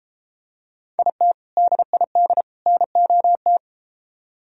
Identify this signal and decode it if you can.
Morse code, heard as a keyed tone, about 32 wpm, 700 Hz. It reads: IT DID NOT